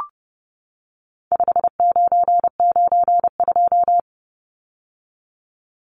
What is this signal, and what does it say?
Morse code, heard as a keyed tone, about 30 wpm, 700 Hz. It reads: 5992
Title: Numbers - 4-Digit 30wpm